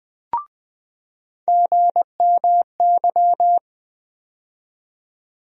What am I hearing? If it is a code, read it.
Morse code, heard as a keyed tone, about 20 wpm, 700 Hz. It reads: GMY
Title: USA Call Sign Postfixes - 3 Letters 20wpm